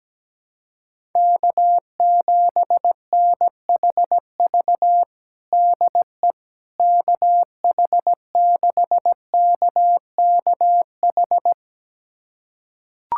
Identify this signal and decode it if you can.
Morse code, heard as a keyed tone, about 17 wpm, 700 Hz. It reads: K7NHV DE KH6KKH
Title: USA Callsign DE Callsign 17wpm